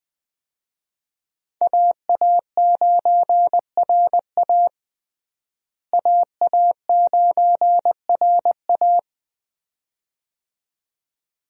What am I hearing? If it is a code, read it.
Morse code, heard as a keyed tone, about 20 wpm, 700 Hz. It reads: AA9RA AA9RA
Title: USA Call Signs - 5 Characters - Repeated 2x - 20wpm